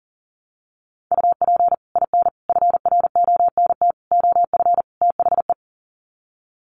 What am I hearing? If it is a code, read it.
Morse code, heard as a keyed tone, about 40 wpm, 700 Hz. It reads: UP IN FRONT OF THE
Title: Sets of 5 Words Encore2 40wpm